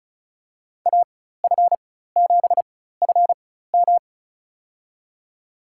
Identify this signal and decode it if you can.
Morse code, heard as a keyed tone, about 36 wpm, spaced as for 16 wpm, 700 Hz. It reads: AF7FM
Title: USA Call Signs - 5 Characters - 4x Character Spacing - 35wpm